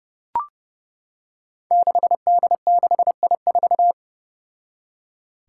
Morse code, heard as a keyed tone, about 29 wpm, 700 Hz. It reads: 6D6I4
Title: Five Challenging Letter-Number Combinations 30wpm